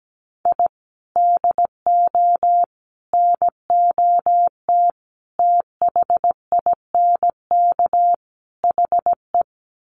Morse code, heard as a keyed tone, about 17 wpm, 700 Hz. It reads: I DO NOT THINK HE
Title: Sets of 5 Words 17wpm